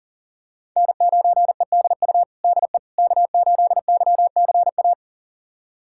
Morse code, heard as a keyed tone, about 40 wpm, 700 Hz. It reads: N9EDU DE K8YCA
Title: USA Callsign DE Callsign - 5 Characters - 40wpm